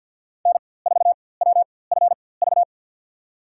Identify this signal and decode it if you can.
Morse code, heard as a keyed tone, about 52 wpm, spaced as for 23 wpm, 700 Hz. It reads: N4WFV
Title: USA Call Signs - 5 Characters - 4x Character Spacing - 50wpm